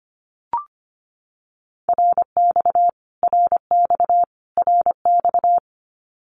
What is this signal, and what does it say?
Morse code, heard as a keyed tone, about 25 wpm, 700 Hz. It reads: RX RX RX
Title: Abbreviations Repeated 3x - 25wpm